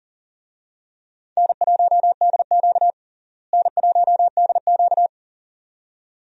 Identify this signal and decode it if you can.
Morse code, heard as a keyed tone, about 40 wpm, 700 Hz. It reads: N1DQ N1DQ
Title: USA Call Signs - 4 Characters - Repeated 2x - 40wpm